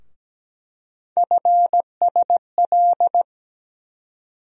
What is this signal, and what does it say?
Morse code, heard as a keyed tone, about 17 wpm, 700 Hz. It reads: FSL
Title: USA Call Sign Postfixes - 3 Letters 17wpm